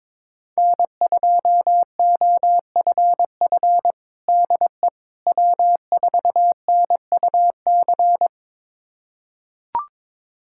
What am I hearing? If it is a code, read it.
Morse code, heard as a keyed tone, about 22 wpm, 700 Hz. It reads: N2OFF DE W4NUC
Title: USA Callsign DE Callsign - 5 Characters - 22wpm